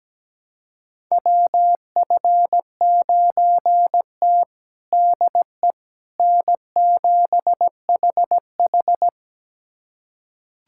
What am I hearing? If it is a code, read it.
Morse code, heard as a keyed tone, about 17 wpm, 700 Hz. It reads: WF9T DE N7HH
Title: USA Callsign DE Callsign - 4 Characters - 17wpm